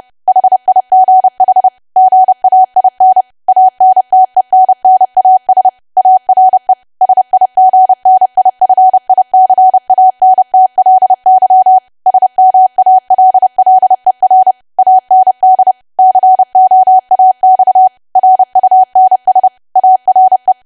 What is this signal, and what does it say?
Morse code, heard as a keyed tone, about 30 wpm, 750 Hz. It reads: HIGH GAIN ANTENNAS ARE SIGNIFICANTLY SMALLER AND COAX RUNS ARE
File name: cwa / W1AW-Level3 / 30WPM